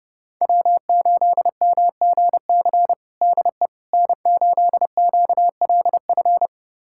Morse code, heard as a keyed tone, about 30 wpm, 700 Hz. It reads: W8MGC DE N8QLF
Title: USA Callsign DE Callsign - 5 Characters - 30wpm